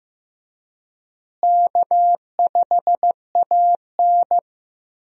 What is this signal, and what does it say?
Morse code, heard as a keyed tone, about 15 wpm, 700 Hz. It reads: K5AN
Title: USA Call Signs - 4 Characters - Repeated 2x - 15wpm